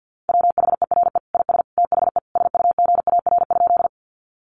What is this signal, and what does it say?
Morse code, heard as a keyed tone, about 50 wpm, 700 Hz. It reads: WHERE IS THE SUGAR?
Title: Sentences from Top 1200 Words 50wpm